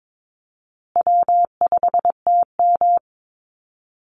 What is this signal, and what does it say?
Morse code, heard as a keyed tone, about 22 wpm, 700 Hz. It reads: W5TM
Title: Call Signs Repeated 2x - 22wpm